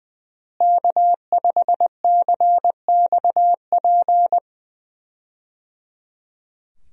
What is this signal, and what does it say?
Morse code, heard as a keyed tone, about 21 wpm, 700 Hz. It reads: K5CXP
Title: USA Call Signs - 5 Characters - Repeated 3x - 20wpm